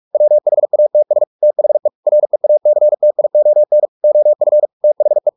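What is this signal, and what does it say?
Morse code, heard as a keyed tone, about 44 wpm, 600 Hz. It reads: WHATS THE REACTION OF THE
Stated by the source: Morse practice at 45 wpm, the novel Never by Ken Follett in English, from CW med Gnister.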